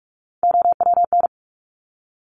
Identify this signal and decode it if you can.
Morse code, heard as a keyed tone, about 46 wpm, 700 Hz. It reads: OWN
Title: Top Three Letter Combinations in Words 45wpm